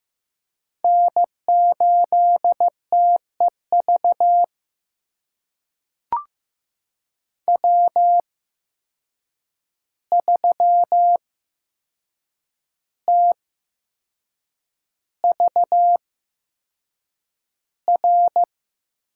Morse code, heard as a keyed tone, about 15 wpm, spaced as for 6 wpm, 700 Hz. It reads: N8TEV W3TVR
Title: USA Call Signs - 5 Characters - 8x Character Spacing - 15wpm